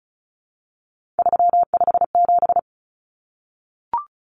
Morse code, heard as a keyed tone, about 34 wpm, 700 Hz. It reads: 357
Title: Common Cut Numbers - 3-Digit 35wpm